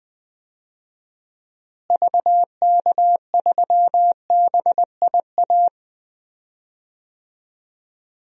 Morse code, heard as a keyed tone, about 20 wpm, 700 Hz. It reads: VK3BIA
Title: DXCC-100 Call Signs - 6 Characters - Repeated 2x - 20wpm